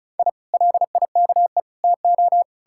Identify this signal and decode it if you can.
Morse code, heard as a keyed tone, about 48 wpm, spaced as for 42 wpm, 700 Hz. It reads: I LIKE TO
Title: Speed-Racing Common QSO Phrases - 35wpm